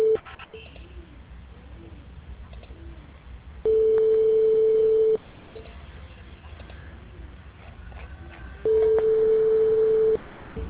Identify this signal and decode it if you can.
Ringback tone (France), 440 Hz.